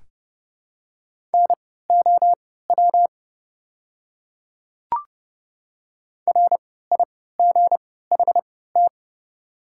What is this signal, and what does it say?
Morse code, heard as a keyed tone, about 31 wpm, spaced as for 17 wpm, 690 Hz. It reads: NOW RIGHT
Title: 3x Character Spacing - Top 200 Words - 30wpm